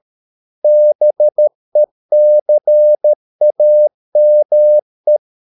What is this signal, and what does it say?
Morse code, heard as a keyed tone, about 13 wpm, 600 Hz. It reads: BECAME